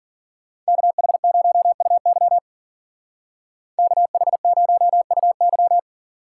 Morse code, heard as a keyed tone, about 49 wpm, 700 Hz. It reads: KH0UY KH0UY
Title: Speed Racing - USA 2x2 Call Signs 40wpm